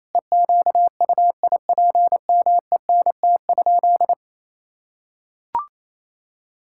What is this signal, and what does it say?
Morse code, heard as a keyed tone, about 28 wpm, 700 Hz. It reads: EQUIPMENT?
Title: Sentences from Top 1000 Words - Review Part 5 - 28wpm